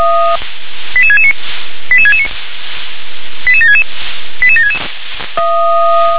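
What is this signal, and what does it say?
Five-tone selective call (ZVEI), heard as quick runs of five tones, tones of about 70 ms. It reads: 88669 88669 88766 88766
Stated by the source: ZVEI five-tone selective call 88690, received on an SDR.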